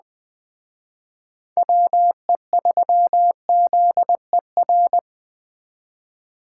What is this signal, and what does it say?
Morse code, heard as a keyed tone, about 20 wpm, 700 Hz. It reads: WE3ZER